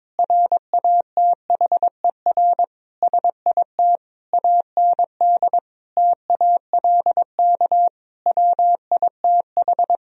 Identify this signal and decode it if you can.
Morse code, heard as a keyed tone, about 22 wpm, 700 Hz. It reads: RATHER SIT AND TALK WITH
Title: Sentences from Top 1000 Words - Review Part 4 - 22wpm